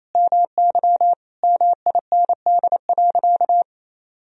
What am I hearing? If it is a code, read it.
Morse code, heard as a keyed tone, about 28 wpm, 700 Hz. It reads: MY MIND.
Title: Sentences from Top 1000 Words - Review Part 1 - 28wpm